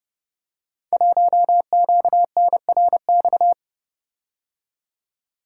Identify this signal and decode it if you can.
Morse code, heard as a keyed tone, about 30 wpm, 700 Hz. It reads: 1QNRX
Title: Code Groups - 5 Letter:Numbers - 30wpm